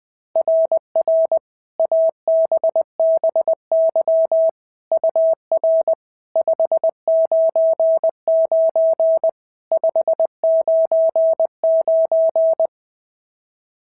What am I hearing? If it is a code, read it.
Morse code, heard as a keyed tone, about 20 wpm, 660 Hz. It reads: RR ABBY UR 599 599
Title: SOTA QSO Practice 20wpm